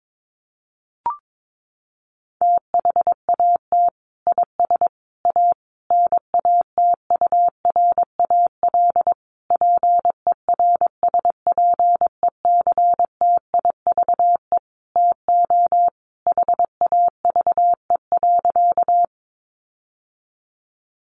Morse code, heard as a keyed tone, about 22 wpm, 700 Hz. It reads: THAT IS A NATURAL PERSPECTIVE TO HAVE.
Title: Sentences from Top 1200 Words 22wpm